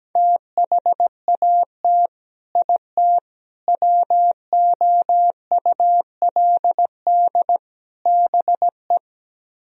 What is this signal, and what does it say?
Morse code, heard as a keyed tone, about 17 wpm, 700 Hz. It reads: THAT IT WOULD BE
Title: Speed-Racing - Sets of 4 Words 17wpm